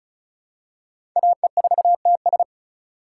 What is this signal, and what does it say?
Morse code, heard as a keyed tone, about 34 wpm, 700 Hz. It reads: AE4TS
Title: USA Call Signs - 5 Characters - 4x Character Spacing - 35wpm